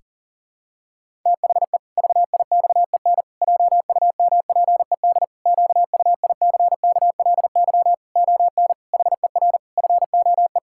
Morse code, heard as a keyed tone, about 40 wpm, 700 Hz. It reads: THE VIXEN JUMPED QUICKLY ON HER FOE
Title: Pangram Challenge v2 - 40wpm